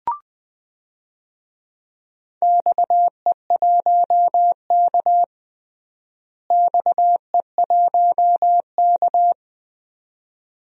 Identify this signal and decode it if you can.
Morse code, heard as a keyed tone, about 19 wpm, 700 Hz. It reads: XE1K XE1K
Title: DXCC-100 Call Signs - 4 Characters - Repeated 2x - 20wpm